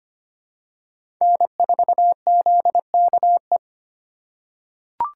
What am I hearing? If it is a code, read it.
Morse code, heard as a keyed tone, about 25 wpm, 700 Hz. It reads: N4ZKE
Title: USA Call Signs - 5 Characters - 2x Character Spacing - 25wpm